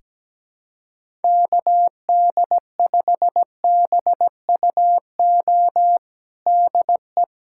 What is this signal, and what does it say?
Morse code, heard as a keyed tone, about 17 wpm, 700 Hz. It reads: KD5BUO DE
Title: USA Callsign DE Callsign - 6 Characters - 17wpm